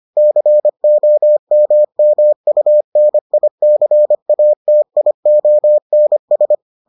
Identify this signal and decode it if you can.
Morse code, heard as a keyed tone, about 25 wpm, 600 Hz. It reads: COMMUNICATIONS